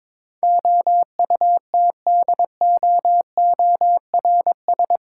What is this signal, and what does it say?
Morse code, heard as a keyed tone, about 22 wpm, 700 Hz. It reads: OUTDOORS
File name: practice / 22wpm / Eight Letter Words - 22wpm